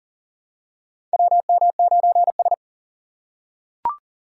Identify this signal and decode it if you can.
Morse code, heard as a keyed tone, about 40 wpm, 700 Hz. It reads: WM9S
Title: USA Call Signs - 4x Character Spacing - 40wpm